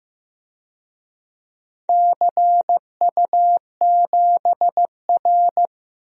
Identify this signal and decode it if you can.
Morse code, heard as a keyed tone, about 15 wpm, 700 Hz. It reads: CU7R
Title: DXCC-100 Call Signs - 4 Characters - 15wpm